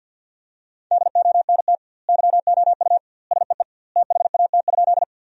Morse code, heard as a keyed tone, about 50 wpm, 700 Hz. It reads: DONT YOU SEE THAT?
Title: 2x Word-Spacing - Sentences from Top 300 Words - 50wpm